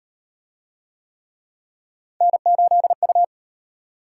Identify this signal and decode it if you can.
Morse code, heard as a keyed tone, about 35 wpm, 700 Hz. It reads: N8U